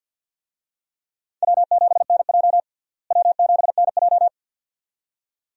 Morse code, heard as a keyed tone, about 52 wpm, 700 Hz. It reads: W7NJ W7NJ
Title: USA Call Signs - 4 Characters - Repeated 2x - 50wpm